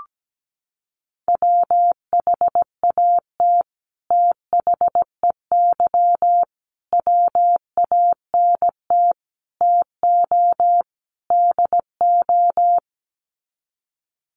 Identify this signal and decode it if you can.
Morse code, heard as a keyed tone, about 17 wpm, 700 Hz. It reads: WHAT THEY WANT TO DO
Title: Sets of 5 Words Encore2 17wpm